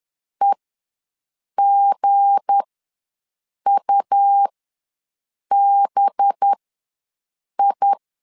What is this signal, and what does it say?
Morse code, heard as a keyed tone, about 10 wpm, spaced as for 6 wpm, 800 Hz. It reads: EGUBI